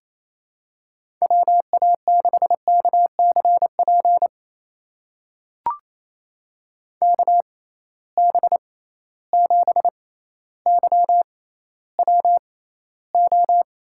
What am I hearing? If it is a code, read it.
Morse code, heard as a keyed tone, about 28 wpm, spaced as for 14 wpm, 700 Hz. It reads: WA6KCP KB7YWO
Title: USA Call Signs - 6 Characters - 6x Character Spacing - 28wpm